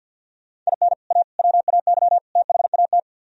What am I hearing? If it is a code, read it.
Morse code, heard as a keyed tone, about 62 wpm, 700 Hz. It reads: IN A WAY THAT
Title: Speed-Racing - Sets of 4 Words 50wpm